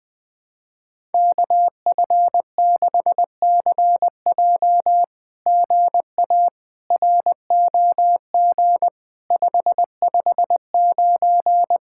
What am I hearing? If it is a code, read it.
Morse code, heard as a keyed tone, about 19 wpm, 700 Hz. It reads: KF6CJ GA ROG 559
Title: SOTA QSO Practice 20wpm